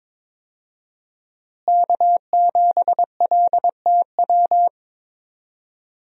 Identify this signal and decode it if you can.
Morse code, heard as a keyed tone, about 22 wpm, 700 Hz. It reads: K7LTW
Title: USA Call Signs - 5 Characters - Repeated 2x - 22wpm